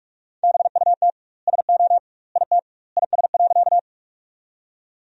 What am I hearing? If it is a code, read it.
Morse code, heard as a keyed tone, about 46 wpm, 700 Hz. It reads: BUT SO IT IS.
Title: 2x Word-Spacing - Sentences from Top 100 Words - 45wpm